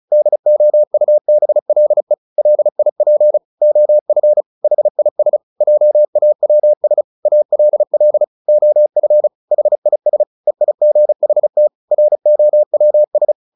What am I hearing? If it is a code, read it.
Morse code, heard as a keyed tone, about 34 wpm, 600 Hz. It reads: DOUBLE LIP OF HIS JAWS ALL OF HIS EIGHT ROWS
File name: the old man and the sea 2 35wpm